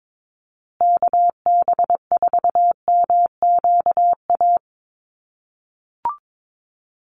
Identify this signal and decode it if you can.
Morse code, heard as a keyed tone, about 22 wpm, 700 Hz. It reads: KB4MQA
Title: USA Call Signs Repeated 3x 22wpm